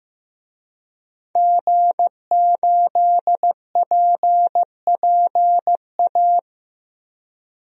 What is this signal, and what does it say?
Morse code, heard as a keyed tone, about 15 wpm, 700 Hz. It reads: G8PPA
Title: DXCC-100 Call Signs - 5 Characters - 15wpm